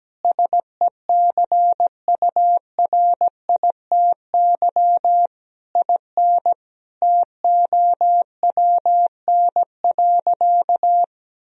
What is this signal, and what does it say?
Morse code, heard as a keyed tone, about 17 wpm, 700 Hz. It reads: SECURITY IN TOWN.
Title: Sentences from Top 1000 Words - Review Part 4 - 17wpm